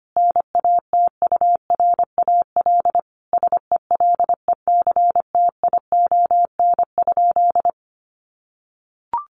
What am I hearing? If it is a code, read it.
Morse code, heard as a keyed tone, about 25 wpm, 700 Hz. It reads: NATURAL SELECTION?
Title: Sentences from Top 1400 Words 25wpm